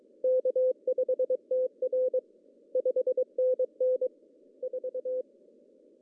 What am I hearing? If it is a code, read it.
Morse code, heard as a keyed tone, about 22 wpm, 510 Hz. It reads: K5TR 5NN 4
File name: CW / cqww-cw-k5tr